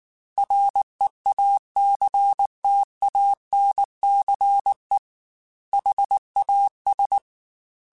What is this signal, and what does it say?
Morse code, heard as a keyed tone, about 19 wpm, 800 Hz. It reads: REACTANCE HAS